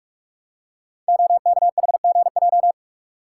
Morse code, heard as a keyed tone, about 44 wpm, 700 Hz. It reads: OKHGJ
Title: Code Groups - 5 Letter:Numbers - 45wpm